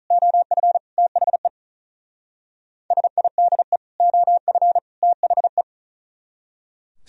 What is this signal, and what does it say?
Morse code, heard as a keyed tone, about 46 wpm, 700 Hz. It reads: OF THE SIDE OF THE